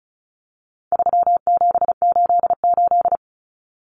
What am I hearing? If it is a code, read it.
Morse code, heard as a keyed tone, about 34 wpm, 700 Hz. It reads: 3788
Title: Common Cut Numbers - 4-Digit 35wpm